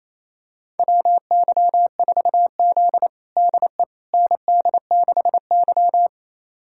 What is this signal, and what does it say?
Morse code, heard as a keyed tone, about 28 wpm, 700 Hz. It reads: WY4Z DE ND6Y
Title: USA Callsign DE Callsign - 4 Characters - 28wpm